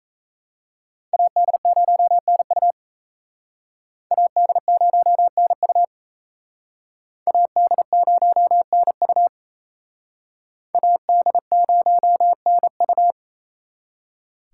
Morse code, heard as a keyed tone, about 35 wpm, 700 Hz. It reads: AD0NU AD0NU AD0NU AD0NU
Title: Speed Racing - USA 2x2 Call Signs 28wpm